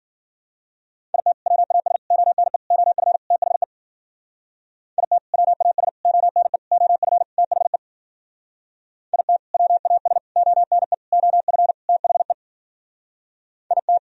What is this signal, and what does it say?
Morse code, heard as a keyed tone, about 52 wpm, 700 Hz. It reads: IT WAS ONE OF THE IT WAS ONE OF THE IT WAS ONE OF THE IT